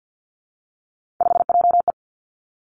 Morse code, heard as a keyed tone, about 50 wpm, 700 Hz. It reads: 5JE
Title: Three Challenging Letter-Number Combinations 50wpm